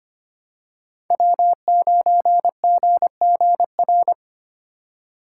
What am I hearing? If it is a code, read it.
Morse code, heard as a keyed tone, about 25 wpm, 700 Hz. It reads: W9GGR
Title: USA Call Signs - 5 Characters - 3x Character Spacing - 25wpm